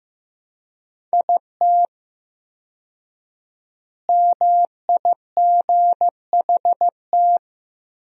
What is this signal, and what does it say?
Morse code, heard as a keyed tone, about 15 wpm, 700 Hz. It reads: IT MIGHT
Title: 4x Word-Spacing - Sentences from Top 300 Words - 15wpm